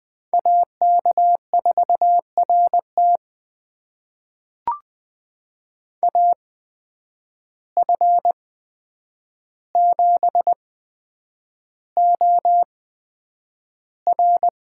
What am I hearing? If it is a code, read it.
Morse code, heard as a keyed tone, about 20 wpm, spaced as for 9 wpm, 700 Hz. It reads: AK4RT AF7OR
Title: USA Call Signs - 8x Character Spacing - 20wpm